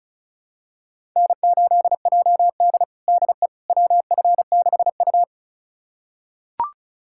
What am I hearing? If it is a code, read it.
Morse code, heard as a keyed tone, about 34 wpm, 700 Hz. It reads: N8JD DE WF6U
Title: USA Callsign DE Callsign - 4 Characters - 35wpm